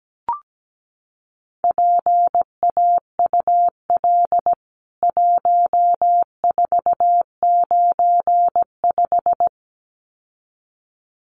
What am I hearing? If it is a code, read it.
Morse code, heard as a keyed tone, about 17 wpm, 700 Hz. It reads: PAUL 1495
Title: CWT Exchange 17wpm